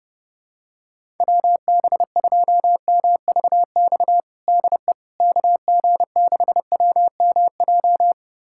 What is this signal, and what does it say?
Morse code, heard as a keyed tone, about 31 wpm, 700 Hz. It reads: WB2MVX DE KG6WMJ
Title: USA Callsign DE Callsign - 6 Characters - 30wpm